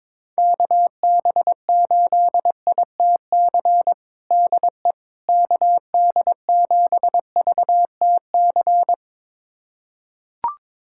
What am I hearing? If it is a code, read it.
Morse code, heard as a keyed tone, about 22 wpm, 700 Hz. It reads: KB8ITC DE KD7VTC
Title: USA Callsign DE Callsign - 6 Characters - 22wpm